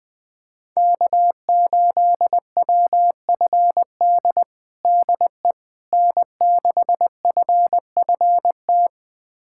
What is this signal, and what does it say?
Morse code, heard as a keyed tone, about 19 wpm, 700 Hz. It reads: K8WFD DE N6FFT
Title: USA Callsign DE Callsign - 5 Characters - 20wpm